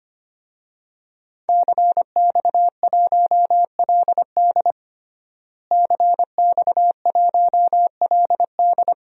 Morse code, heard as a keyed tone, about 25 wpm, 700 Hz. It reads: CX1LD CX1LD
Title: DXCC-100 Call Signs - 5 Characters - Repeated 3x - 25wpm